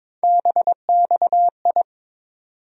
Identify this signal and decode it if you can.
Morse code, heard as a keyed tone, about 21 wpm, 700 Hz. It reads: BXI
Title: USA Call Sign Postfixes - 3 Letters 22wpm